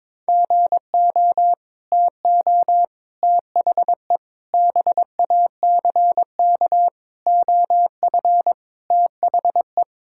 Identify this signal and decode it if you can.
Morse code, heard as a keyed tone, about 22 wpm, 700 Hz. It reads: GO TO THE BACK OF THE
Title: Sentences from Top 1000 Words - Review Part 5 - 22wpm